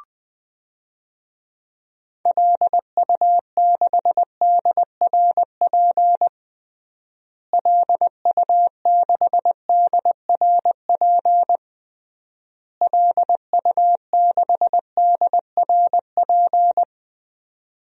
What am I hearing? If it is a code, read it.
Morse code, heard as a keyed tone, about 21 wpm, 700 Hz. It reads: LU6DRP LU6DRP LU6DRP
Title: DXCC-100 Call Signs - 6 Characters - Repeated 3x - 20wpm